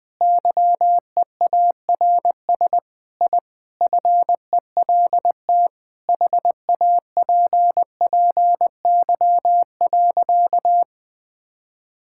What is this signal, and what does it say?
Morse code, heard as a keyed tone, about 20 wpm, 700 Hz. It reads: YEARS I FELT HAPPY.
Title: Sentences with 5-Word Phrases - 20wpm